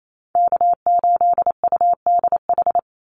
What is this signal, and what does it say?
Morse code, heard as a keyed tone, about 28 wpm, 700 Hz. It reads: K8UDH
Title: Call Signs Repeated 3x - 28wpm